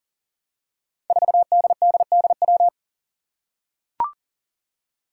Morse code, heard as a keyed tone, about 40 wpm, 700 Hz. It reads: 4DDDW
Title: Five Challenging Letter-Number Combinations 40wpm